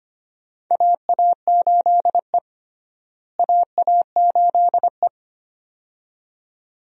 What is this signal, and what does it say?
Morse code, heard as a keyed tone, about 25 wpm, 700 Hz. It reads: AA8E AA8E
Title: USA Call Signs - 4 Characters - Repeated 2x - 25wpm